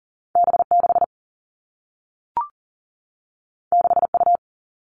Morse code, heard as a keyed tone, about 39 wpm, 700 Hz. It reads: B 6 6 U
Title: Two Challenging Letter-Number Combinations 40wpm